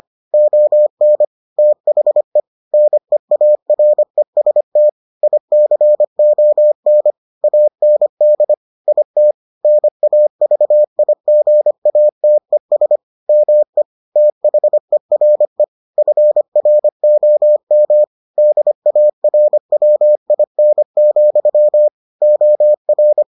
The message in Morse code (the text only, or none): ON THE NEAREST ICON AND IT NAVIGATES ME THERE FROM DARWIN, OR